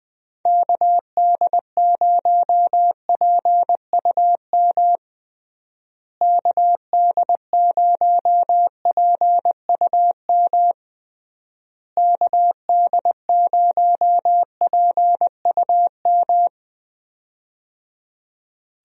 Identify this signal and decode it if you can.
Morse code, heard as a keyed tone, about 20 wpm, 700 Hz. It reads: KD0PUM KD0PUM KD0PUM
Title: USA Call Signs - 6 Characters - Repeated 3x - 20wpm